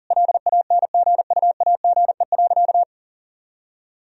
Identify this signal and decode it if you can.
Morse code, heard as a keyed tone, about 40 wpm, 700 Hz. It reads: LANGUAGE.
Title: Sentences from Top 1000 Words - Review Part 3 - 40wpm